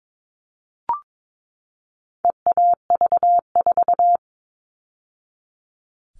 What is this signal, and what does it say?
Morse code, heard as a keyed tone, about 22 wpm, 700 Hz. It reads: EAV4
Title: All Cut Numbers - 4-Digit 22wpm